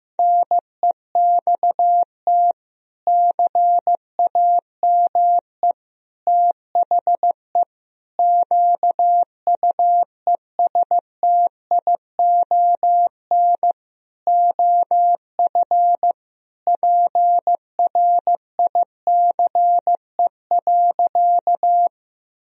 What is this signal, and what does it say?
Morse code, heard as a keyed tone, about 16 wpm, 700 Hz. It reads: NEXT CAME THE QUESTION OF PRICE.
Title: Sentences from Top 1000 Words - Review Part 4 - 15wpm